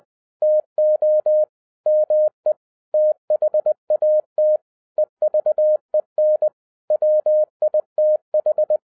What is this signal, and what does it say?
Morse code, heard as a keyed tone, about 20 wpm, 610 Hz. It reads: TO ME THAT EVEN WITH
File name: 2012-01-mp3 / n05-CAME BACK TO HAM RADIO AFTER 30 YEARS OF INTERVAL-20wpm